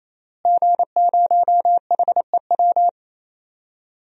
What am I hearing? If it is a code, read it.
Morse code, heard as a keyed tone, about 28 wpm, 690 Hz. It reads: G0HEW
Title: DXCC-100 Call Signs - 5 Characters - Repeated 3x - 28wpm